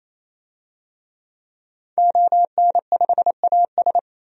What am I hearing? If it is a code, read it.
Morse code, heard as a keyed tone, about 28 wpm, 700 Hz. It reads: ON5AS
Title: DXCC-100 Call Signs - 5 Characters - 28wpm